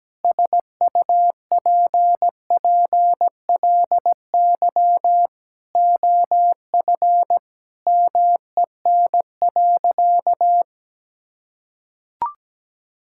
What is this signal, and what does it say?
Morse code, heard as a keyed tone, about 17 wpm, 700 Hz. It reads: SUPPLY OF MEN.
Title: Sentences from Top 1000 Words - Review Part 5 - 17wpm